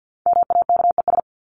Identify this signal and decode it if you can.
Morse code, heard as a keyed tone, about 50 wpm, 700 Hz. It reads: MAKES
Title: ES:S Suffix - Top 100 Words with Base Words 50wpm